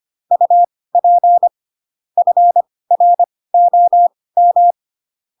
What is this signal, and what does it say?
Morse code, heard as a keyed tone, about 25 wpm, spaced as for 18 wpm, 700 Hz. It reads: UP FROM
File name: Pinocchio / Pinocchio ch7 12 25 ik1hge